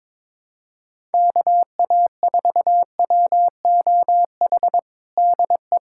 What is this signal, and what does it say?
Morse code, heard as a keyed tone, about 21 wpm, 700 Hz. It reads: KA4WOH DE
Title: USA Callsign DE Callsign 22wpm